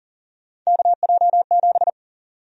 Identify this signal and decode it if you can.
Morse code, heard as a keyed tone, about 40 wpm, 700 Hz. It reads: KJ7
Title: USA Call Sign Prefixes - Repeated 2x - 40wpm